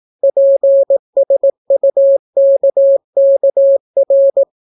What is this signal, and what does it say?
Morse code, heard as a keyed tone, about 18 wpm, 550 Hz. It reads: PSUKKR